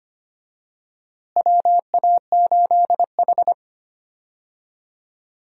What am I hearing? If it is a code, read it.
Morse code, heard as a keyed tone, about 25 wpm, 700 Hz. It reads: WA8H
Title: USA Call Signs - 4 Characters - Repeated 3x - 25wpm